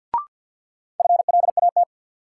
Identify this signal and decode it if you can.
Morse code, heard as a keyed tone, about 52 wpm, 700 Hz. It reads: FLAT
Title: Sentences from Top 1000 Words - New Word Drill - 50wpm